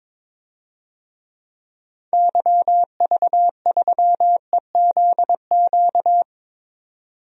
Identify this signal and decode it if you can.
Morse code, heard as a keyed tone, about 22 wpm, 700 Hz. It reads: YV3EZQ
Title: DXCC-100 Call Signs - 6 Characters - 22wpm